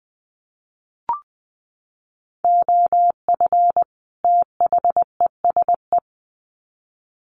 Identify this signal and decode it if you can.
Morse code, heard as a keyed tone, about 20 wpm, 700 Hz. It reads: OF THESE
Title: Sets of 2 Words 20wpm